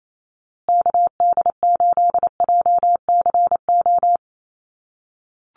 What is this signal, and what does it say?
Morse code, heard as a keyed tone, about 28 wpm, 700 Hz. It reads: KD8JCO
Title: USA Call Signs Repeated 3x 28wpm